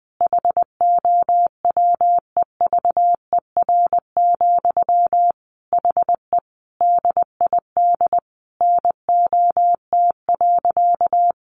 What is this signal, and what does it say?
Morse code, heard as a keyed tone, about 20 wpm, 700 Hz. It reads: HOWEVER, HE DID NOT.
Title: Sentences from Top 600 Words 20wpm